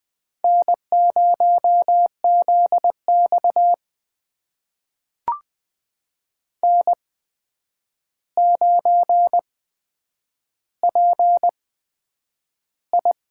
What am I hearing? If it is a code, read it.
Morse code, heard as a keyed tone, about 20 wpm, spaced as for 9 wpm, 700 Hz. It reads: N0ZX N9PI
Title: USA Call Signs - 4 Characters - 8x Character Spacing - 20wpm